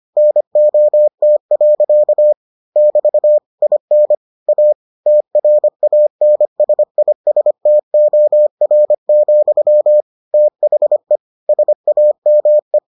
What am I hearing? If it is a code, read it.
Morse code, heard as a keyed tone, about 25 wpm, 600 Hz. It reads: NOT. <BT> IN A TRANSISTOR, THE SAME